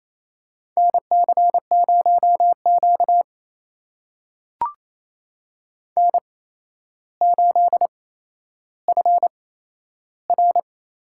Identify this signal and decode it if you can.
Morse code, heard as a keyed tone, about 27 wpm, spaced as for 12 wpm, 700 Hz. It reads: NC0Q N8FR
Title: USA Call Signs - 4 Characters - 8x Character Spacing - 28wpm